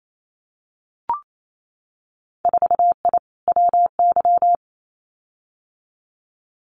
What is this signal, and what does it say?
Morse code, heard as a keyed tone, about 28 wpm, 700 Hz. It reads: 4I WY
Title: Winter Field Day Exchanges Encore 28wpm